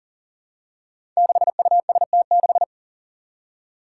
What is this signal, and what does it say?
Morse code, heard as a keyed tone, about 40 wpm, 700 Hz. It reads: 6UST6
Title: Five Challenging Letter-Number Combinations 40wpm